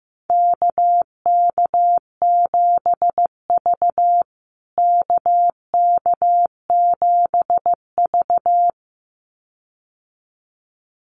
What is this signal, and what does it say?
Morse code, heard as a keyed tone, about 14 wpm, 700 Hz. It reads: KK7V KK7V
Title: USA Call Signs Repeated 2x 15wpm